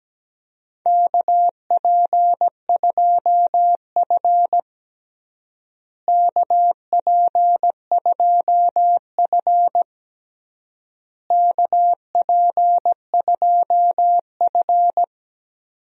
Morse code, heard as a keyed tone, about 17 wpm, 700 Hz. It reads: KP2F KP2F KP2F
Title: DXCC-100 Call Signs - 4 Characters - Repeated 3x - 17wpm